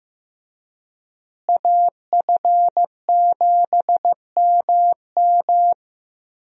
Morse code, heard as a keyed tone, about 15 wpm, 700 Hz. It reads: AF7MM
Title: USA Call Signs - 5 Characters - Repeated 2x - 15wpm